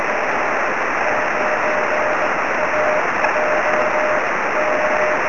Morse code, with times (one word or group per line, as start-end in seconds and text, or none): none